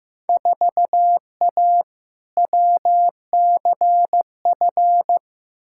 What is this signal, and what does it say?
Morse code, heard as a keyed tone, about 15 wpm, 700 Hz. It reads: 4A WCF
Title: Field Day Exchanges - 15wpm